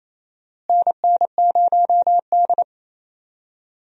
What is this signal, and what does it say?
Morse code, heard as a keyed tone, about 28 wpm, 700 Hz. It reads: NN0D
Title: USA Call Signs - 4 Characters - Repeated 2x - 28wpm